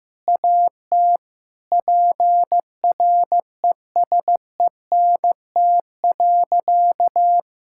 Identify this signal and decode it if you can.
Morse code, heard as a keyed tone, about 16 wpm, 700 Hz. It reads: AT PRESENT.
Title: Sentences from Top 1000 Words - Review Part 5 - 15wpm